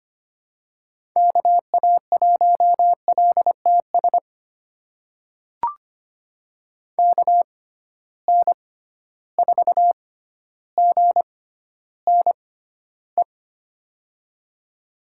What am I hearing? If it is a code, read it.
Morse code, heard as a keyed tone, about 25 wpm, spaced as for 13 wpm, 700 Hz. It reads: KA1LTS KN4GNE
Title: USA Call Signs - 6x Character Spacing - 25wpm